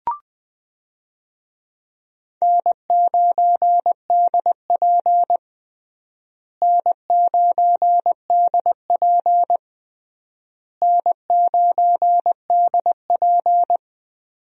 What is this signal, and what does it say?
Morse code, heard as a keyed tone, about 21 wpm, 700 Hz. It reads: N9DP N9DP N9DP
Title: USA Call Signs - 4 Characters - Repeated 3x - 20wpm